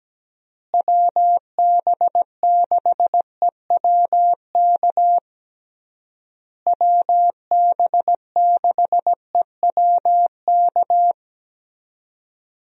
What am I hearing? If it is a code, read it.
Morse code, heard as a keyed tone, about 17 wpm, 700 Hz. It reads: WB6EWK WB6EWK
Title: USA Call Signs - 6 Characters - Repeated 3x - 17wpm